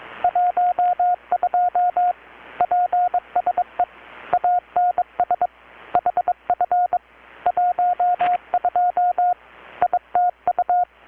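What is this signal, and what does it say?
Morse code, heard as a keyed tone, about 22 wpm, 720 Hz, with radio noise. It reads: 12 PSE ANS HF 12 ITU
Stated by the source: Morse from KPH's Night of Nights XX maritime radiotelegraphy event, received in Chiapas, Mexico about 3,800 km away on a Kenwood R-600 with a grounded 10 m long-wire antenna.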